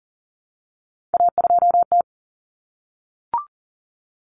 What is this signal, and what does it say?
Morse code, heard as a keyed tone, about 40 wpm, 700 Hz. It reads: A2T
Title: Common Cut Numbers - 3-Digit 40wpm